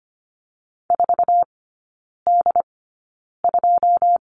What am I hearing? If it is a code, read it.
Morse code, heard as a keyed tone, about 25 wpm, spaced as for 9 wpm, 700 Hz. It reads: 4D2
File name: Lesson 14e - Number 2 - Three Characters 25wpm